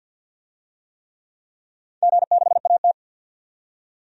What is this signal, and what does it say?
Morse code, heard as a keyed tone, about 51 wpm, 700 Hz. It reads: G6AT
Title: DXCC-100 Call Signs - 4 Characters - 50wpm